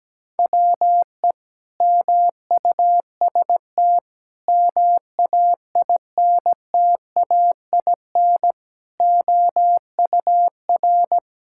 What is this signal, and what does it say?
Morse code, heard as a keyed tone, about 17 wpm, 700 Hz. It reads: WE MUST MAINTAIN OUR
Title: Sentences from Top 1000 Words - Review Part 5 - 17wpm